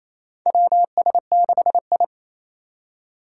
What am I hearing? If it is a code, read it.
Morse code, heard as a keyed tone, about 28 wpm, 700 Hz. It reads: WS6I
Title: USA Call Signs - 4 Characters - 4x Character Spacing - 28wpm